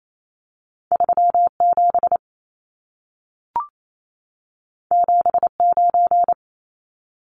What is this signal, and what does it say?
Morse code, heard as a keyed tone, about 28 wpm, 700 Hz. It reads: 37 79
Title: Numbers - 2-Digit 28wpm